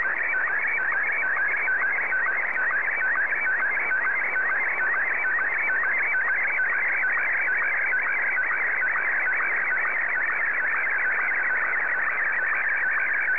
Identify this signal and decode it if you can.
SSTV (slow-scan television), Martin 1.
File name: sstv / M1-JA3QJU